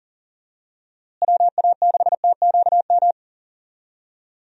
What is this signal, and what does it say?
Morse code, heard as a keyed tone, about 40 wpm, 700 Hz. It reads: WA6TQM
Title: USA Call Signs - 6 Characters - Repeated 2x - 40wpm